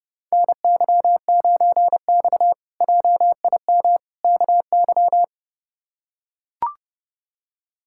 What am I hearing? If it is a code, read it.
Morse code, heard as a keyed tone, about 30 wpm, 700 Hz. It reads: NY9X JIM KY
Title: K1USN SST QSO Practice - 30wpm